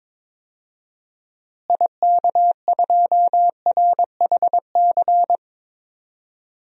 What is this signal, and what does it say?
Morse code, heard as a keyed tone, about 22 wpm, 700 Hz. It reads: IK2RHC
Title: DXCC-100 Call Signs - 6 Characters - Repeated 2x - 22wpm